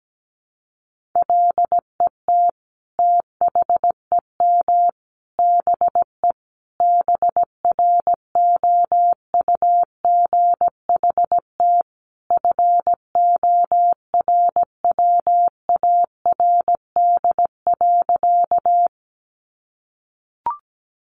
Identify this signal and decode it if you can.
Morse code, heard as a keyed tone, about 17 wpm, 700 Hz. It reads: LET THEM BE BROUGHT FORWARD.
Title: Sentences from Top 1300 Words 17wpm